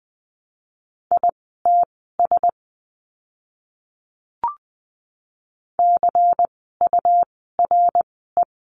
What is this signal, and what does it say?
Morse code, heard as a keyed tone, about 20 wpm, spaced as for 14 wpm, 700 Hz. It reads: ITS CURE
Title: 101 Lesson 17f - Letter u - Words 20wpm